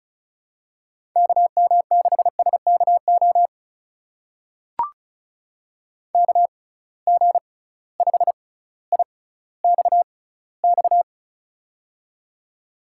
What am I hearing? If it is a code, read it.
Morse code, heard as a keyed tone, about 36 wpm, spaced as for 18 wpm, 700 Hz. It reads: KM6SKO KG5IXX
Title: USA Call Signs - 6 Characters - 6x Character Spacing - 35wpm